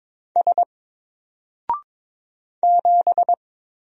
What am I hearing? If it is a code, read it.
Morse code, heard as a keyed tone, about 22 wpm, 700 Hz. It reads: S 7
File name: Single-Letter-Number-22wpm